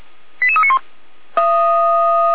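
Five-tone selective call (ZVEI), heard as quick runs of five tones, tones of about 70 ms. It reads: 88391
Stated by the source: ZVEI five-tone selective call received with an SDR.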